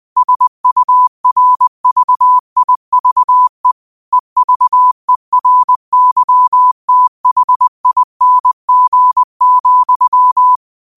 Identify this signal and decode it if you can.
Morse code, heard as a keyed tone, about 20 wpm, 1000 Hz. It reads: SURVIVE EVERYTHING,
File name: qotdfiles / 20wpm